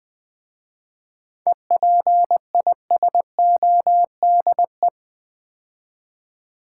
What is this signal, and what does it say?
Morse code, heard as a keyed tone, about 20 wpm, 700 Hz. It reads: EPISODE
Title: 7 Letter Words - First 3 Letters Given - 20wpm